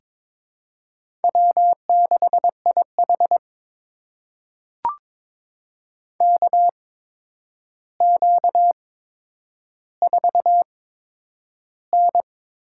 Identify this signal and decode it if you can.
Morse code, heard as a keyed tone, about 22 wpm, spaced as for 9 wpm, 700 Hz. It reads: W6IH KQ4N
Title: USA Call Signs - 4 Characters - 8x Character Spacing - 22wpm